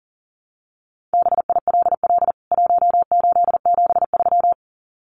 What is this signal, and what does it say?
Morse code, heard as a keyed tone, about 40 wpm, 700 Hz. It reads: BILL 1873
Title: CWT Exchange 40wpm